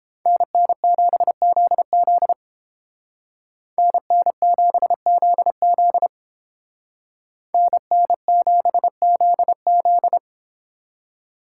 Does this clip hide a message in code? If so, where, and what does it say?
Yes: Morse code from 0.3 to 10.2 s, heard as a keyed tone, about 29 wpm, 700 Hz. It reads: NN7ZZ NN7ZZ NN7ZZ